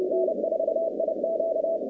Morse code, heard as a keyed tone, about 30 wpm, 650 Hz, with noise. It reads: N4IQ
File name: contests / 2019-cw-wpx-kt5j-k5pi- -2330 0-47